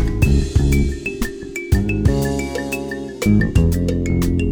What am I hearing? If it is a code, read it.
There is no code.